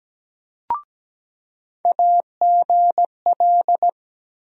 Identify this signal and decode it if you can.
Morse code, heard as a keyed tone, about 17 wpm, 700 Hz. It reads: AGL
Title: USA Call Sign Postfixes - 3 Letters 17wpm